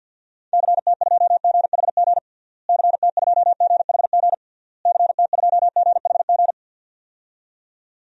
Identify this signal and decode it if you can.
Morse code, heard as a keyed tone, about 50 wpm, 700 Hz. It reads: CT2GHG CT2GHG CT2GHG
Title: DXCC-100 Call Signs - 6 Characters - Repeated 3x - 50wpm